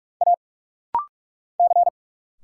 Morse code, heard as a keyed tone, about 46 wpm, 690 Hz. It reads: A C
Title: Single Letters - Rapid-Fire 45wpm